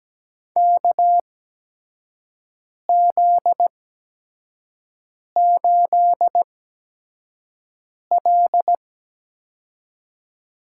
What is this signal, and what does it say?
Morse code, heard as a keyed tone, about 17 wpm, spaced as for 5 wpm, 700 Hz. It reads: KZ8L
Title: USA Call Signs - 4 Characters - 8x Character Spacing - 17wpm